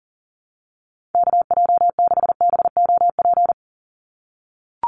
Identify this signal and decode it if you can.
Morse code, heard as a keyed tone, about 40 wpm, 700 Hz. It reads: KJ6BOP
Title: USA Call Signs 40wpm